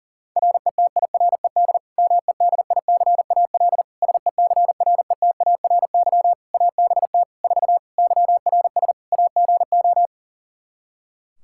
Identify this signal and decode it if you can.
Morse code, heard as a keyed tone, about 40 wpm, 700 Hz. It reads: RETIRED MEDICAL SECRETARY ABT 4 YRS AGO
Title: Practice QSO - Exchange 3 - More Info - 40wpm